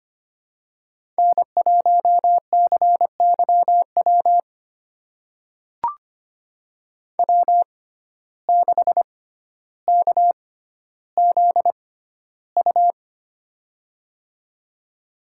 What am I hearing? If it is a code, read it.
Morse code, heard as a keyed tone, about 25 wpm, spaced as for 13 wpm, 700 Hz. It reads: N1CYW W6KZU
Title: USA Call Signs - 5 Characters - 6x Character Spacing - 25wpm